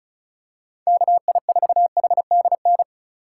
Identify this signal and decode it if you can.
Morse code, heard as a keyed tone, about 34 wpm, 700 Hz. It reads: KI4HDN
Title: USA Call Signs - 6 Characters - 2x Character Spacing - 35wpm